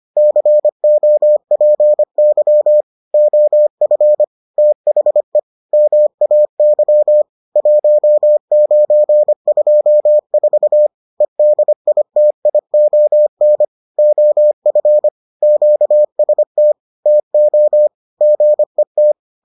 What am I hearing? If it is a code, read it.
Morse code, heard as a keyed tone, about 25 wpm, 600 Hz. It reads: COPY OF THE MAY 1924 EDITION OF QST TO GET